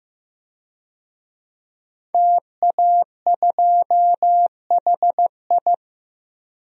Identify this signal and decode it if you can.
Morse code, heard as a keyed tone, about 15 wpm, 700 Hz. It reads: TA2HI
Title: DXCC-100 Call Signs - 5 Characters - 15wpm